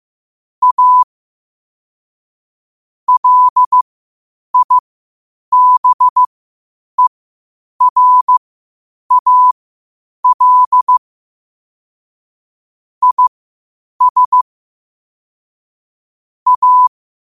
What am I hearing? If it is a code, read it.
Morse code, heard as a keyed tone, about 15 wpm, spaced as for 8 wpm, 1000 Hz. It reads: A LIBERAL IS A
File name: qotdfiles / 8wpm